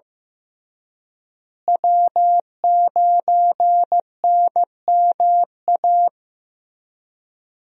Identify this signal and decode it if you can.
Morse code, heard as a keyed tone, about 15 wpm, 700 Hz. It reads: W9NMA
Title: USA Call Signs - 5 Characters - Repeated 2x - 15wpm